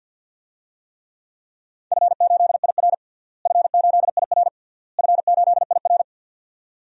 Morse code, heard as a keyed tone, about 47 wpm, 700 Hz. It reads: F8IR F8IR F8IR
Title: DXCC-100 Call Signs - 4 Characters - Repeated 3x - 50wpm